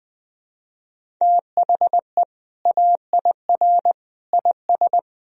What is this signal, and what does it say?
Morse code, heard as a keyed tone, about 21 wpm, 700 Hz. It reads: THE AIR IS
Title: Sentences from Top 1000 Words - Review Part 1 - 20wpm